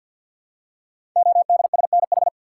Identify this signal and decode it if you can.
Morse code, heard as a keyed tone, about 52 wpm, 700 Hz. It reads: ODSNH
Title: Code Groups - 5 Letter:Numbers - 50wpm